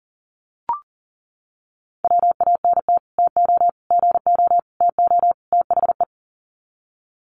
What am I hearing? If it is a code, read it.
Morse code, heard as a keyed tone, about 40 wpm, 700 Hz. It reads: WANT TO GO TO THE
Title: Sets of 5 Words Encore2 40wpm